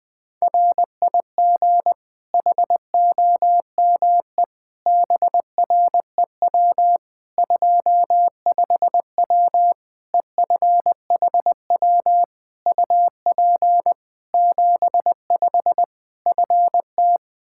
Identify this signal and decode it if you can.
Morse code, heard as a keyed tone, about 20 wpm, 700 Hz. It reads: RIG HOME BREW 25W EFHW UP 75 FT
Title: K1USN SST QSO Practice 20wpm